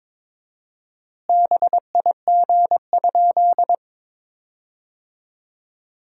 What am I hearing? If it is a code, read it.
Morse code, heard as a keyed tone, about 22 wpm, 700 Hz. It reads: BIG?
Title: 4x Word-Spacing - Sentences from Top 200 Words - 22wpm